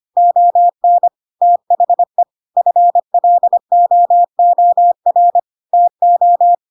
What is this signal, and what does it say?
Morse code, heard as a keyed tone, about 25 wpm, 700 Hz. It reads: ON THE FLOOR TO
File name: Pinocchio / Pinocchio ch3 25 25 ik1hge